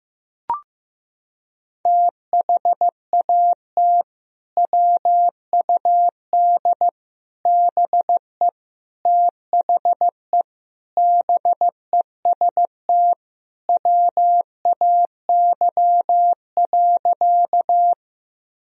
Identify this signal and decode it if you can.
Morse code, heard as a keyed tone, about 16 wpm, 700 Hz. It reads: THAT WUD BE THE BEST WAY.
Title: Sentences from Top 400 Words and Abbreviations 15wpm